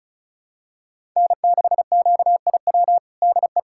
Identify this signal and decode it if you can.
Morse code, heard as a keyed tone, about 36 wpm, 700 Hz. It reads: N6QIW DE
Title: USA Callsign DE Callsign - 5 Characters - 35wpm